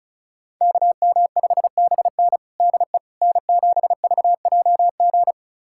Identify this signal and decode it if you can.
Morse code, heard as a keyed tone, about 36 wpm, 700 Hz. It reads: KM5BN DE N7VJG
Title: USA Callsign DE Callsign - 5 Characters - 35wpm